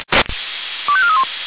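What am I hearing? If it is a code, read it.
Five-tone selective call (ZVEI), heard as quick runs of five tones, tones of about 70 ms. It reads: 25431
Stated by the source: ZVEI five-tone selective call received on an SDR.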